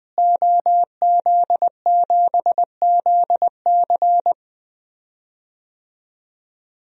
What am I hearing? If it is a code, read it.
Morse code, heard as a keyed tone, about 20 wpm, 700 Hz. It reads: OZ7ZC
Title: DXCC-100 Call Signs - 5 Characters - Repeated 2x - 20wpm